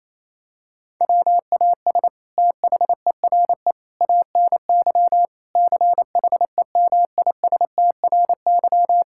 Morse code, heard as a keyed tone, about 28 wpm, 700 Hz. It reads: WAS THERE ANY CHEMISTRY
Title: Sentences from Top 1000 Words - Review Part 6 - 28wpm